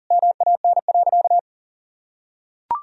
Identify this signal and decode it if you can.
Morse code, heard as a keyed tone, about 40 wpm, 700 Hz. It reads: MAN.
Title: Sentences from Top 200 Words and Abbreviations 40wpm